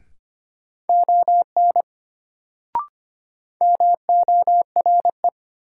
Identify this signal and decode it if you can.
Morse code, heard as a keyed tone, about 25 wpm, 700 Hz. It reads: ON MORE
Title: Sentences from Top 100 Words - New Word Drill - 25wpm